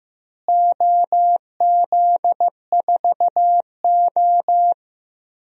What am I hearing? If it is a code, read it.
Morse code, heard as a keyed tone, about 15 wpm, 700 Hz. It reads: OZ4O
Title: DXCC-100 Call Signs - 4 Characters - Repeated 2x - 15wpm